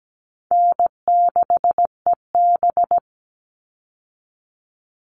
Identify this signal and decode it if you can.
Morse code, heard as a keyed tone, about 17 wpm, 700 Hz. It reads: N6EB
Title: All Cut Numbers - 4-Digit 17wpm